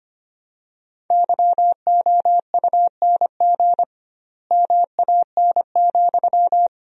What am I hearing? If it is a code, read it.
Morse code, heard as a keyed tone, about 25 wpm, 700 Hz. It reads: YOUNG MAN,
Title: 2x Word-Spacing - Sentences from Top 400 Words - 25wpm